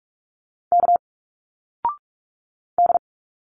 Morse code, heard as a keyed tone, about 46 wpm, 700 Hz. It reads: K D
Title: Single Letters 45wpm